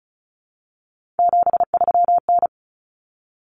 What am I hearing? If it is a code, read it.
Morse code, heard as a keyed tone, about 34 wpm, 700 Hz. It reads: 73N
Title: Common Cut Numbers - 3-Digit 35wpm